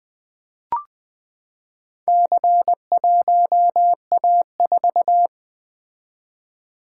Morse code, heard as a keyed tone, about 20 wpm, 700 Hz. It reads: C1A4
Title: Four Letter-Number Combinations - 20wpm